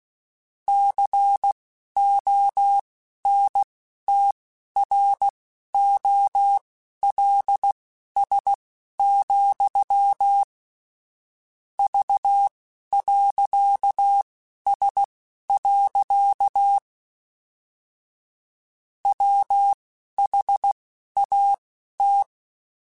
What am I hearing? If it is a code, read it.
Morse code, heard as a keyed tone, about 16 wpm, spaced as for 11 wpm, 800 Hz. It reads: CONTROLS, V.S. WHAT
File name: Technical Subjects / Technical Subjects-mp3s / OR Gate @10